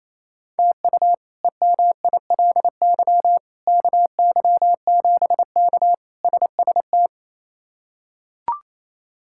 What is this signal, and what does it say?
Morse code, heard as a keyed tone, about 28 wpm, 700 Hz. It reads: TU EMILY KY7K SST
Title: K1USN SST QSO Practice - 28wpm